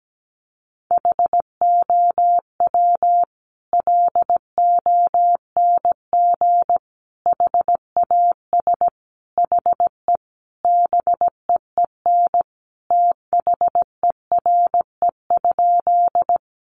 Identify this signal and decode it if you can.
Morse code, heard as a keyed tone, about 17 wpm, 700 Hz. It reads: HOW LONG HAS HE BEEN THERE?
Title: Sentences from Top 100 Words 17wpm